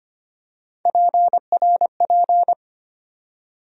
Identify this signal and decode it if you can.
Morse code, heard as a keyed tone, about 25 wpm, 700 Hz. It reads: PRP
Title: USA Call Sign Postfixes - 3 Letters 25wpm